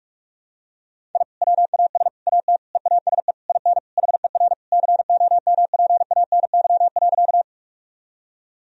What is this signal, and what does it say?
Morse code, heard as a keyed tone, about 44 wpm, 700 Hz. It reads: I WAS AT EASE IN HER COMPANY.
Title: Sentences from Top 1000 Words - Review Part 1 - 45wpm